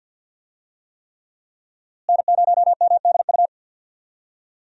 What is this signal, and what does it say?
Morse code, heard as a keyed tone, about 51 wpm, 700 Hz. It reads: N0MDU
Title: USA Call Signs - 5 Characters - Repeated 3x - 50wpm